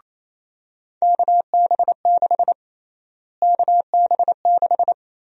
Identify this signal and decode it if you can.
Morse code, heard as a keyed tone, about 28 wpm, 700 Hz. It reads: KB6 KB6
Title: USA Call Sign Prefixes - Repeated 2x - 28wpm